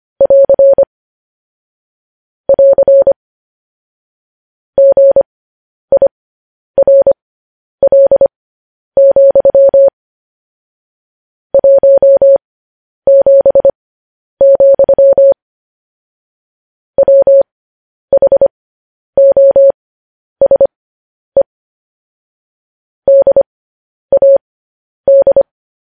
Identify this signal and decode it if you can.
Morse code, heard as a keyed tone, about 25 wpm, spaced as for 10 wpm, 550 Hz. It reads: <AR> <AR> GIRL, 17, WHOSE DAD
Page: news headlines in morse code at 10 wpm